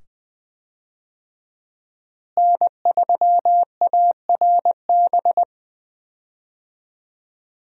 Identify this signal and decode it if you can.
Morse code, heard as a keyed tone, about 20 wpm, 700 Hz. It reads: N3ARB
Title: USA Call Signs - 5 Characters - Repeated 2x - 20wpm